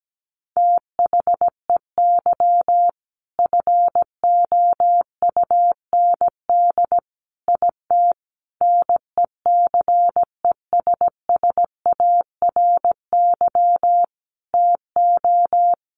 Morse code, heard as a keyed tone, about 17 wpm, 700 Hz. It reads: THEY FOUND IT NECESSARY TO
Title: Sentences from Top 1400 Words 17wpm